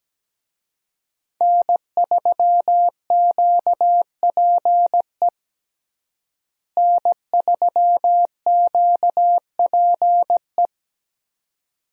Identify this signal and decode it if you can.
Morse code, heard as a keyed tone, about 17 wpm, 700 Hz. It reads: N3QPE N3QPE
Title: USA Call Signs - 5 Characters - Repeated 2x - 17wpm